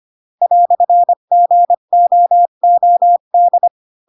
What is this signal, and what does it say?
Morse code, heard as a keyed tone, about 25 wpm, 700 Hz. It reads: "GOOD